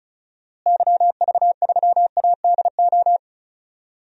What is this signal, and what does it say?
Morse code, heard as a keyed tone, about 36 wpm, 700 Hz. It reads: YV3ADO
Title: DXCC-100 Call Signs - 6 Characters - Repeated 3x - 35wpm